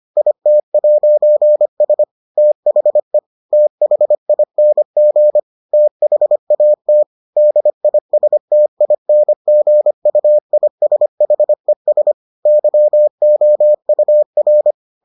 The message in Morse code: IT'S THE THING THAT DISTINGUISHES YOUR

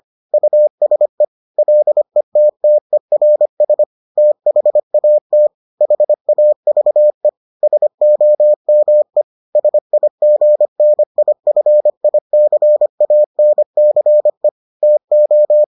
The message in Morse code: USE LETTERS THAT HAVE SOME SIGNIFICANCE TO

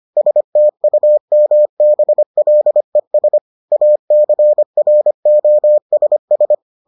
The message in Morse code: STUMBLES ACROSS

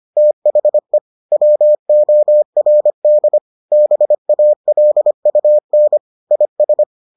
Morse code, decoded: THE WORD BALUN IS

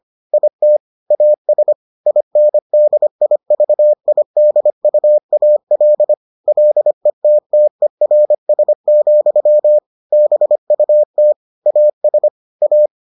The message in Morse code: IT AS INDIVIDUAL LETTERS, BUT AS A

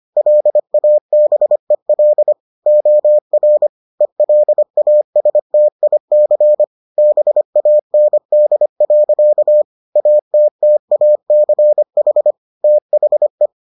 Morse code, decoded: LABEL OR ELASTIC BAND. ATTACH THE

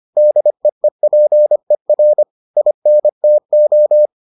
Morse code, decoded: DEEPER INTO

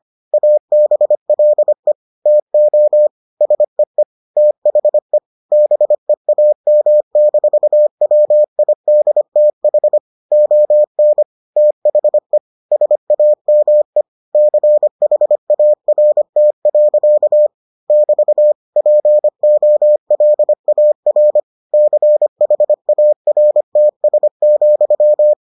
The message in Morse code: ABLE TO SEE THE BEAM-WIDTH ON THE SAME CHART. <BT> POLAR CHARTS,

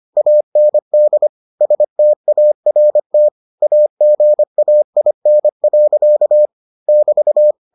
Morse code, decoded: AND START AGAIN. <BT>

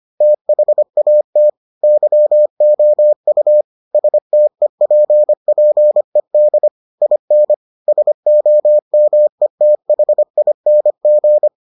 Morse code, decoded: THAT YOU STEPPED IN SOMETHING

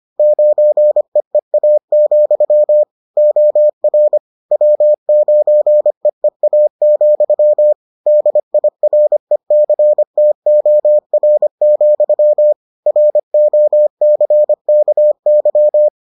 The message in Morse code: 9EEA, OR W9EEA, DIRECTOR, ROCKY

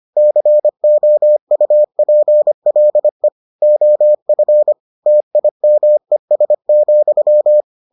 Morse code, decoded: COUPLE OF TIMES,

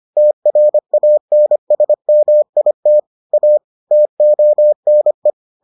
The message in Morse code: TRANSMIT A TONE

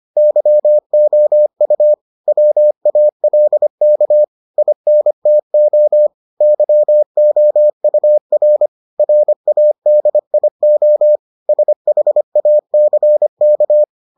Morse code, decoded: YOU WALK INTO YOUR RADIO SHACK

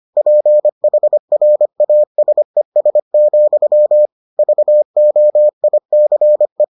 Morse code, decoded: PHRASES, VOICE

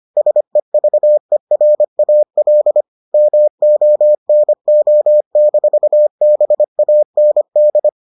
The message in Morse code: SEVERAL MONO-BAND

